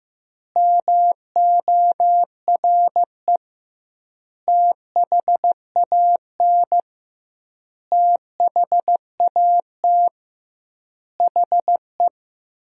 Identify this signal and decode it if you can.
Morse code, heard as a keyed tone, about 15 wpm, 700 Hz. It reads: MORE THAN THAT HE
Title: 2x Word-Spacing - Sentences from Top 200 Words - 15wpm